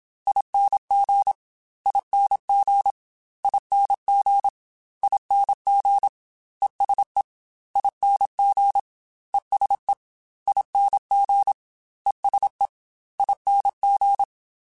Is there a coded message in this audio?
Yes: Morse code from 0.3 to 14.2 s, heard as a keyed tone, about 27 wpm, 800 Hz. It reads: ING ING ING ING ESE ING ESE ING ESE ING